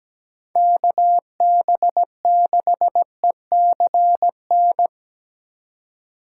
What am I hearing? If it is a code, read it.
Morse code, heard as a keyed tone, about 17 wpm, 700 Hz. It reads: KB6ECN
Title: USA Call Signs - 6 Characters - 3x Character Spacing - 17wpm